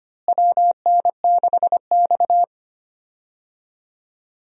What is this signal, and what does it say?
Morse code, heard as a keyed tone, about 25 wpm, 700 Hz. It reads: WN6X
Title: USA Call Signs - 4 Characters - Repeated 3x - 25wpm